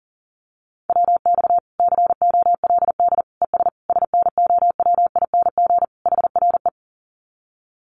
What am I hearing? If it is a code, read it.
Morse code, heard as a keyed tone, about 37 wpm, 700 Hz. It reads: WX COLD ES SNOWING HRE
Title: Common QSO Phrases 40wpm